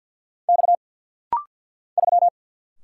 Morse code, heard as a keyed tone, about 48 wpm, 700 Hz. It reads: X 3
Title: Single Letter-Number - Rapid-Fire 50wpm